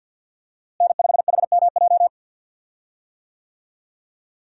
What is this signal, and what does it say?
Morse code, heard as a keyed tone, about 52 wpm, 700 Hz. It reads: N5HMJ
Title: USA Call Signs - 5 Characters - Repeated 3x - 50wpm